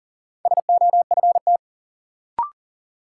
Morse code, heard as a keyed tone, about 40 wpm, 700 Hz. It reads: SOFT
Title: Sentences from Top 1500 Words - New Word Drill Alt - 40wpm